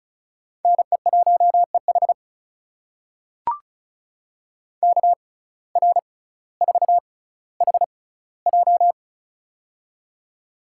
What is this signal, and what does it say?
Morse code, heard as a keyed tone, about 36 wpm, spaced as for 18 wpm, 700 Hz. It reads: NE1EH KR4HJ
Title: USA Call Signs - 5 Characters - 6x Character Spacing - 35wpm